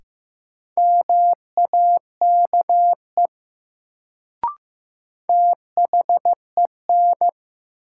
Morse code, heard as a keyed tone, about 16 wpm, 700 Hz. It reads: MAKE THEN
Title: Sentences from Top 100 Words - New Word Drill - 15wpm